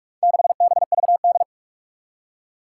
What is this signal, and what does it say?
Morse code, heard as a keyed tone, about 46 wpm, 700 Hz. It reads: 6BVD
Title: Four Challenging Letter-Number Combinations 45wpm